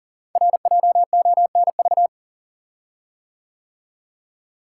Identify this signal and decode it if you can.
Morse code, heard as a keyed tone, about 39 wpm, 690 Hz. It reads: RJONV
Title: Code Groups - 5 Letters - 40wpm